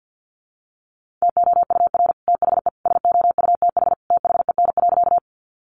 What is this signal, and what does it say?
Morse code, heard as a keyed tone, about 50 wpm, 700 Hz. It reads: TOUR THE SOUTH THEN.
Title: Sentences from Top 1400 Words 50wpm